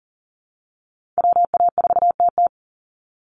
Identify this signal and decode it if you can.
Morse code, heard as a keyed tone, about 40 wpm, 700 Hz. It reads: WA4TT
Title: Call Signs Repeated 3x - 40wpm